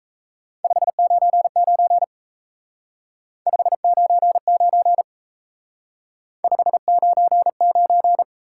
Morse code, heard as a keyed tone, about 32 wpm, 700 Hz. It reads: 599 599 599
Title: Speed-Racing Top 180 QSO Elements 28wpm